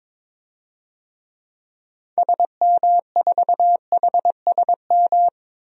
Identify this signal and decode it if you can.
Morse code, heard as a keyed tone, about 22 wpm, 700 Hz. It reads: SM4HSM
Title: DXCC-100 Call Signs - 6 Characters - 22wpm